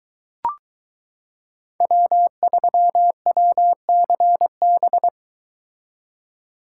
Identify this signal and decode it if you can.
Morse code, heard as a keyed tone, about 22 wpm, 700 Hz. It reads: W3WCB